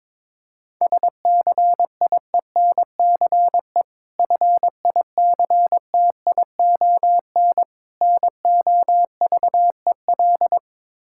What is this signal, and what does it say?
Morse code, heard as a keyed tone, about 22 wpm, 690 Hz. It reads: SCIENCE FICTION NOVEL
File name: Difficult Words with Hints - 22wpm